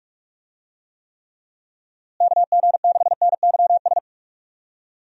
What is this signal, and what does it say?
Morse code, heard as a keyed tone, about 46 wpm, 700 Hz. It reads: KG6NYS
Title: USA Call Signs - 6 Characters - Repeated 3x - 45wpm